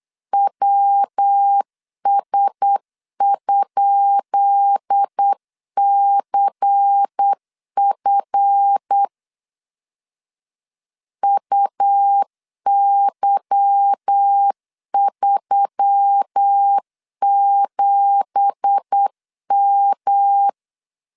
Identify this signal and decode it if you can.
Morse code, heard as a keyed tone, about 8 wpm, 800 Hz. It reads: WS?CF UY37M
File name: LEZIONI / 10 SIGNS / 10B INT 8wpm 3534